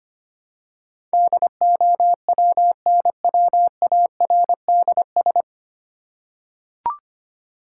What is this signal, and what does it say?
Morse code, heard as a keyed tone, about 25 wpm, 690 Hz. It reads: DOWNWARDS
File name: practice / 25wpm / 9 Letter Words - First 3 Letters Given - 25wpm